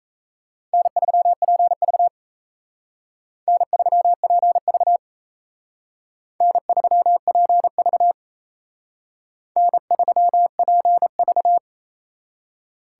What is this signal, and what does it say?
Morse code, heard as a keyed tone, about 34 wpm, 700 Hz. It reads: N3PV N3PV N3PV N3PV
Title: Speed Racing - USA Call Signs 28wpm